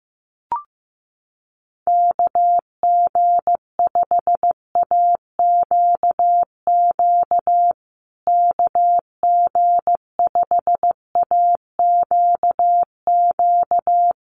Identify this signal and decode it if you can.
Morse code, heard as a keyed tone, about 15 wpm, 700 Hz. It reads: KG5AQQ KG5AQQ
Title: USA Call Signs Repeated 3x 15wpm